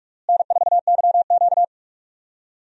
Morse code, heard as a keyed tone, about 46 wpm, 700 Hz. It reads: N4YQ
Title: USA Call Signs - 4 Characters - Repeated 2x - 45wpm